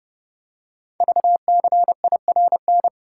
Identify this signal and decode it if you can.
Morse code, heard as a keyed tone, about 29 wpm, 700 Hz. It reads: VCIRN
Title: Code Groups - 5 Letter:Numbers - 30wpm